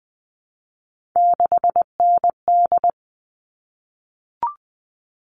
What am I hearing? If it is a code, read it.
Morse code, heard as a keyed tone, about 20 wpm, 700 Hz. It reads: 6ND
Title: All Cut Numbers - 3-Digit 20wpm